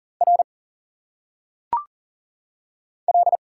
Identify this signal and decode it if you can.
Morse code, heard as a keyed tone, about 40 wpm, 700 Hz. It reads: R L
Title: Single Letters - Phonetic Pronunciation 40wpm